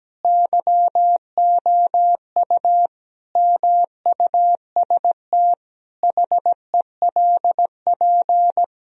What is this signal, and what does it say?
Morse code, heard as a keyed tone, about 17 wpm, 700 Hz. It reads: YOU MUST HELP
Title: Sentences from Top 1000 Words - Review Part 1 - 17wpm